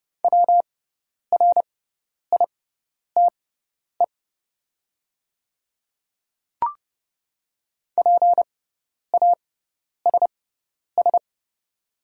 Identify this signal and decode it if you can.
Morse code, heard as a keyed tone, about 30 wpm, spaced as for 10 wpm, 700 Hz. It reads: WRITE PASS
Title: 6x Character Spacing - Top 500 Words - 30wpm